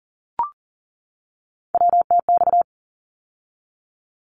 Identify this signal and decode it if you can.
Morse code, heard as a keyed tone, about 42 wpm, 700 Hz. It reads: WTX
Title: ARRL Sections 40wpm